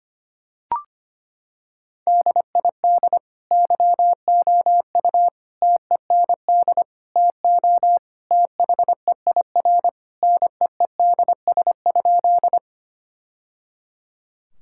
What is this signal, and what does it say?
Morse code, heard as a keyed tone, about 25 wpm, 700 Hz. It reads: DID YOU TEND TO THEIR NEEDS?
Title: Sentences from Top 1000 Words - Review Part 5 - 25wpm